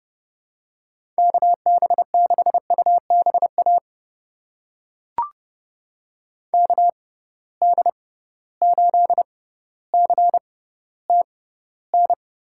Morse code, heard as a keyed tone, about 30 wpm, spaced as for 15 wpm, 700 Hz. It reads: KB6UBA KD8CTN
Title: USA Call Signs - 6 Characters - 6x Character Spacing - 30wpm